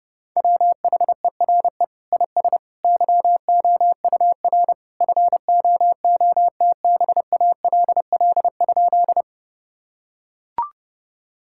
Morse code, heard as a keyed tone, about 32 wpm, 700 Hz. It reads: WHERE IS YOUR FOOTBALL?
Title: Sentences from Top 1600 Words 30wpm